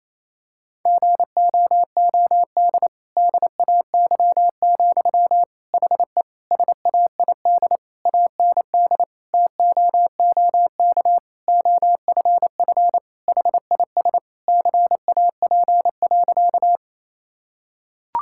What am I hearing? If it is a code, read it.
Morse code, heard as a keyed tone, about 28 wpm, 700 Hz. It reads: GOOD DAY, HE SAID AND TOOK OFF HIS CAP.
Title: Sentences from Top 1600 Words 28wpm